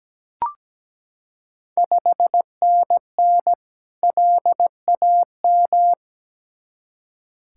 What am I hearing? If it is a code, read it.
Morse code, heard as a keyed tone, about 17 wpm, 690 Hz. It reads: 5NN LAM
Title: Ontario QSO Party Exchange 17wpm